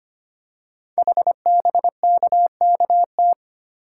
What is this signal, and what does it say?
Morse code, heard as a keyed tone, about 25 wpm, 700 Hz. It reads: HBKKT
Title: Code Groups - 5 Letters - 25wpm